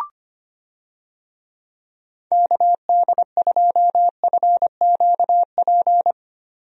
Morse code, heard as a keyed tone, about 25 wpm, 700 Hz. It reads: KD2FQP
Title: USA Call Signs - 6 Characters - Repeated 3x - 25wpm